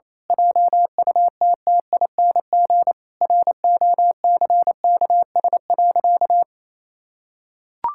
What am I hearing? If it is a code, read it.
Morse code, heard as a keyed tone, about 28 wpm, 700 Hz. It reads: JUTTING ROCKS.
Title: Pangram Challenge v2 - 28wpm